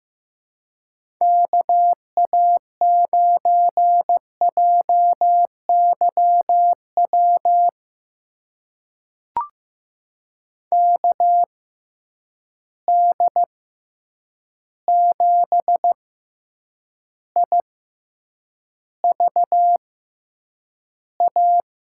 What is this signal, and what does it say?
Morse code, heard as a keyed tone, about 15 wpm, spaced as for 8 wpm, 700 Hz. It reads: KA9JYW KD7IVA
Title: USA Call Signs - 6x Character Spacing - 15wpm